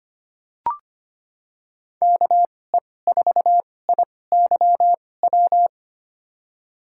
Morse code, heard as a keyed tone, about 25 wpm, spaced as for 18 wpm, 700 Hz. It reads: KE4IYW
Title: USA Call Signs - 6 Characters - 2x Character Spacing - 25wpm